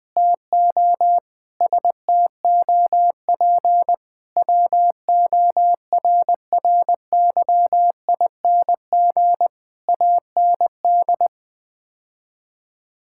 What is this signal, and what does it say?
Morse code, heard as a keyed tone, about 20 wpm, 700 Hz. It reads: TO STOP WORRYING AND
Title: Sets of 8 Words - 20wpm